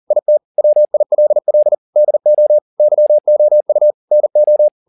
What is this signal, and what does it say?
Morse code, heard as a keyed tone, about 40 wpm, 600 Hz. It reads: IT WILL DO YOU NO